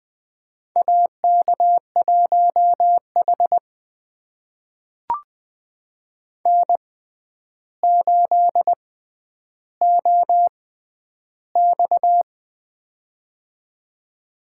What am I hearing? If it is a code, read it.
Morse code, heard as a keyed tone, about 20 wpm, spaced as for 10 wpm, 700 Hz. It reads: AK1H N8OX
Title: USA Call Signs - 4 Characters - 6x Character Spacing - 20wpm